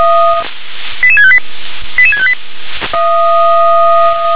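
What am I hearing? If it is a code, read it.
Five-tone selective call (ZVEI), heard as quick runs of five tones, tones of about 70 ms. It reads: 88658 88658
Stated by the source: ZVEI five-tone call 88655 received on SDR.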